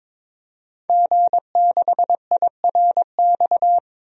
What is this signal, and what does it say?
Morse code, heard as a keyed tone, about 22 wpm, 700 Hz. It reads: G6IRX
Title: DXCC-100 Call Signs - 5 Characters - 22wpm